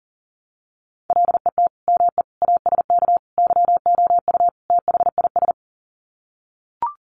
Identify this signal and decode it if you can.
Morse code, heard as a keyed tone, about 40 wpm, 700 Hz. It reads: LET ME ASK YOU THIS
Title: Sets of 5 Words 40wpm